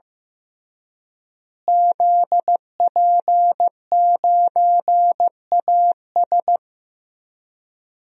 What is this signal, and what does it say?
Morse code, heard as a keyed tone, about 15 wpm, 700 Hz. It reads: ZP9AS
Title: DXCC-100 Call Signs - 5 Characters - Repeated 3x - 15wpm